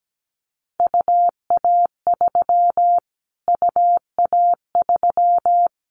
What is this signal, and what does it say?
Morse code, heard as a keyed tone, about 17 wpm, 700 Hz. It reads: UA3 UA3
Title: Callsign Prefixes for Top 25 DXCC - Repeated 3x - 17wpm